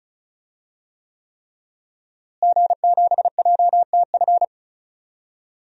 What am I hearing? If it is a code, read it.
Morse code, heard as a keyed tone, about 36 wpm, 700 Hz. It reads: G7JTF
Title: DXCC-100 Call Signs - 5 Characters - Repeated 3x - 35wpm